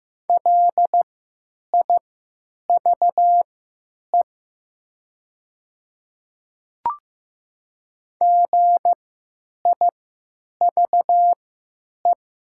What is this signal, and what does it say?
Morse code, heard as a keyed tone, about 15 wpm, spaced as for 9 wpm, 700 Hz. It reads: LIVE GIVE
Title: 3x Character Spacing - Top 200 Words - 15wpm